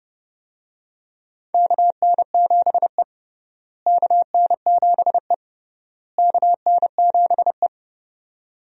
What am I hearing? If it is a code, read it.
Morse code, heard as a keyed tone, about 30 wpm, 700 Hz. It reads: KN7E KN7E KN7E
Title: USA Call Signs - 4 Characters - Repeated 3x - 30wpm